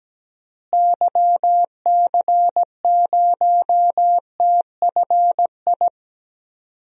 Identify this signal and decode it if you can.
Morse code, heard as a keyed tone, about 17 wpm, 700 Hz. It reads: YC0TFI
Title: DXCC-100 Call Signs - 6 Characters - Repeated 3x - 17wpm